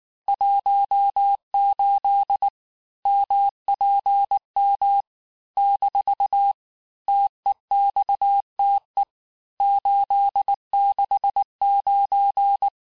Morse code, heard as a keyed tone, about 18 wpm, 800 Hz. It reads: 18 MPM - TEXTE 869
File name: cours-du-mercredi / audios / cours-869-texte-CHIFFRE-vitesse-18-MPM